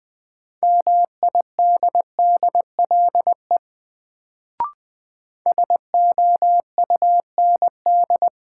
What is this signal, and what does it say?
Morse code, heard as a keyed tone, about 21 wpm, 700 Hz. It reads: MIDDLE SOUND
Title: Sentences from Top 900 Words - New Word Drill - 20wpm